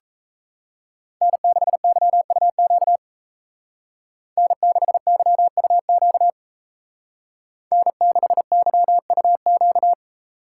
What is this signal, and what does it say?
Morse code, heard as a keyed tone, about 37 wpm, 700 Hz. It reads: N6YUQ N6YUQ N6YUQ